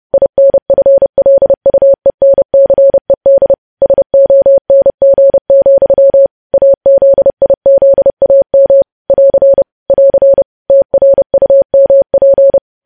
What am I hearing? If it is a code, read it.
Morse code, heard as a keyed tone, about 30 wpm, 550 Hz. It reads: INFLUENCED SONG, AZIZAM <AR> <AR> TRUMP
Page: News Headlines in Morse Code at 30 WPM